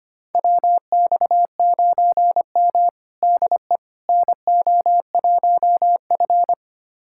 Morse code, heard as a keyed tone, about 25 wpm, 700 Hz. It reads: WX9M DE NO1F
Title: USA Callsign DE Callsign - 4 Characters - 25wpm